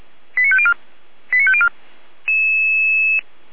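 Five-tone selective call (ZVEI), heard as quick runs of five tones, tones of about 70 ms. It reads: 89504 89504
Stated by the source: ZVEI five-tone selective call received by SDR, from audio.f85360.feuerwehr-orlamuende.org fire brigade server.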